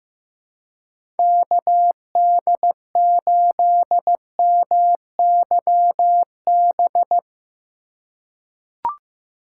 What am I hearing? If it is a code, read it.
Morse code, heard as a keyed tone, about 15 wpm, 700 Hz. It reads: KD8MYB
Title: USA Call Signs - 6 Characters - 3x Character Spacing - 15wpm